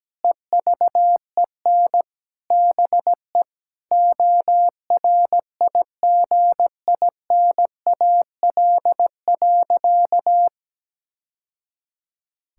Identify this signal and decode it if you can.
Morse code, heard as a keyed tone, about 17 wpm, 700 Hz. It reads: EVEN BE ORIGINAL.
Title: Sentences from Top 1000 Words - Review Part 6 - 17wpm